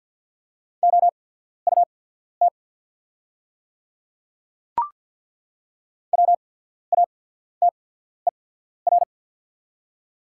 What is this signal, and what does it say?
Morse code, heard as a keyed tone, about 48 wpm, spaced as for 14 wpm, 690 Hz. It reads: OUT WATER
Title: 8x Character Spacing - Top 100 Words - 50wpm